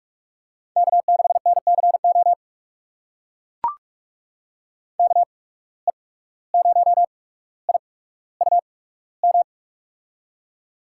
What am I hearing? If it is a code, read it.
Morse code, heard as a keyed tone, about 43 wpm, spaced as for 18 wpm, 700 Hz. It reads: K6NCO KE0IUM
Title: USA Call Signs - 8x Character Spacing - 45wpm